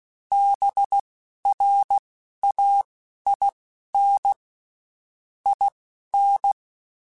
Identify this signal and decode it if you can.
Morse code, heard as a keyed tone, about 16 wpm, spaced as for 11 wpm, 800 Hz. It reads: BRAIN IN
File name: Spaced / Chuckles Spaced-mp3s / BrainMouth sp @15